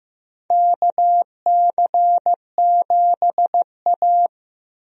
Morse code, heard as a keyed tone, about 15 wpm, 700 Hz. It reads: KC7A
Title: USA Call Signs - 4 Characters - 15wpm